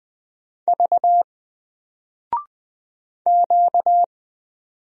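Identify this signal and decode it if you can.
Morse code, heard as a keyed tone, about 20 wpm, 700 Hz. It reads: V Q
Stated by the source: Single letter and number Morse drill at 20 wpm, from N4LCD.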